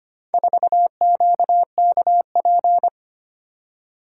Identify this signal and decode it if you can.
Morse code, heard as a keyed tone, about 25 wpm, 700 Hz. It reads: 4QKP
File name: practice / 25wpm / Four Letter-Number Combinations - 25wpm